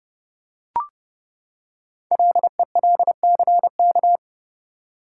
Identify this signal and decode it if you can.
Morse code, heard as a keyed tone, about 30 wpm, 700 Hz. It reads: LELCK
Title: Code Groups - 5 Letters - 30wpm